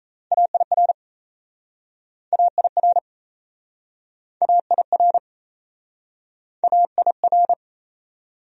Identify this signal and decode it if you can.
Morse code, heard as a keyed tone, about 39 wpm, 690 Hz. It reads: AIR AIR AIR AIR